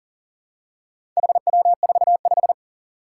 Morse code, heard as a keyed tone, about 42 wpm, 700 Hz. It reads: HW45
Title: Four Challenging Letter-Number Combinations 40wpm